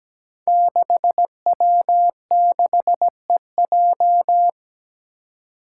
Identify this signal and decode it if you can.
Morse code, heard as a keyed tone, about 17 wpm, 700 Hz. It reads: 6W6EJ
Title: Five Challenging Letter-Number Combinations 17wpm